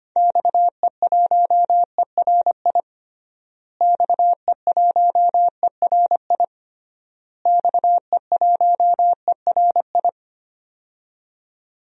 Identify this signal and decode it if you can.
Morse code, heard as a keyed tone, about 25 wpm, 700 Hz. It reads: XE1ERI XE1ERI XE1ERI
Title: DXCC-100 Call Signs - 6 Characters - Repeated 3x - 25wpm